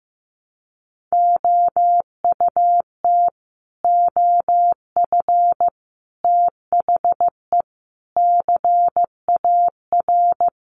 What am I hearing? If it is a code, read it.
Morse code, heard as a keyed tone, about 14 wpm, 700 Hz. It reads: OUT OF THE CAR
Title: Sets of 5 Words Encore 15wpm